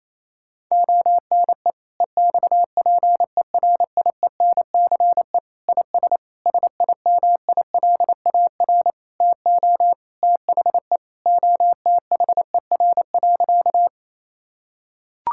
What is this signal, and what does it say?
Morse code, heard as a keyed tone, about 28 wpm, 700 Hz. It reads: ONE EXPERIENCE IS SIMILAR TO THE OTHER.
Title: Sentences from Top 1000 Words - Review Part 5 - 28wpm